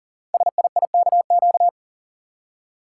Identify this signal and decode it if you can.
Morse code, heard as a keyed tone, about 40 wpm, 690 Hz. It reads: SIIKQ
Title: Code Groups - 5 Letters - 40wpm